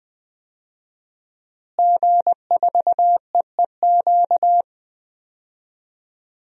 Morse code, heard as a keyed tone, about 20 wpm, 700 Hz. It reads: G4EEQ
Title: DXCC-100 Call Signs - 5 Characters - Repeated 2x - 20wpm